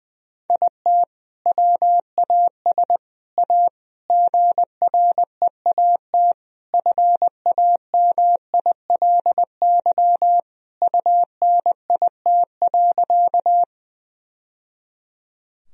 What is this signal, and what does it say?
Morse code, heard as a keyed tone, about 20 wpm, 700 Hz. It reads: IT WAS A GREAT FAMILY UNIT.
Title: Sentences from Top 1000 Words - Review Part 2 - 20wpm